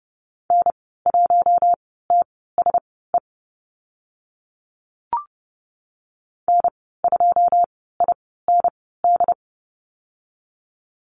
Morse code, heard as a keyed tone, about 30 wpm, spaced as for 17 wpm, 700 Hz. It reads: N1TSE N2IND
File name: Lesson 14g - Number 2 - Call Signs 30wpm